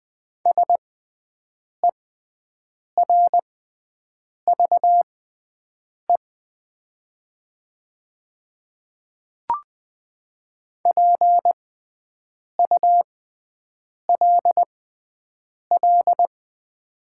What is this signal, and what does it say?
Morse code, heard as a keyed tone, about 20 wpm, spaced as for 7 wpm, 700 Hz. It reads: SERVE PULL
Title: 6x Character Spacing - Top 500 Words - 20wpm